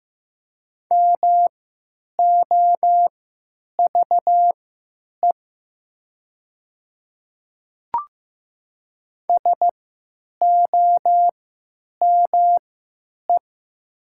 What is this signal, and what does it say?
Morse code, heard as a keyed tone, about 15 wpm, spaced as for 9 wpm, 700 Hz. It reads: MOVE SOME
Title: 3x Character Spacing - Top 500 Words - 15wpm